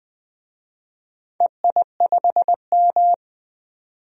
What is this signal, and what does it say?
Morse code, heard as a keyed tone, about 21 wpm, 700 Hz. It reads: EI5M
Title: DXCC-100 Call Signs - 4 Characters - 20wpm